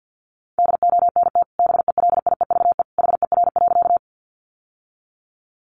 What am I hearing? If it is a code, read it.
Morse code, heard as a keyed tone, about 50 wpm, 700 Hz. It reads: DONT BELIEVE HER.
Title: Sentences from Top 1000 Words 50wpm